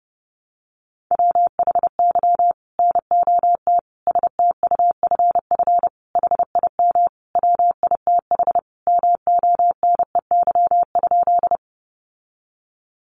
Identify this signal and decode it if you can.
Morse code, heard as a keyed tone, about 30 wpm, 700 Hz. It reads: WHY NOT STUFF HIM WITH MONEY?
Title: Sentences from Top 1400 Words 30wpm